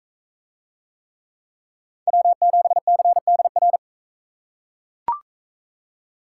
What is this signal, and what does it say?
Morse code, heard as a keyed tone, about 42 wpm, 700 Hz. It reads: W7CDR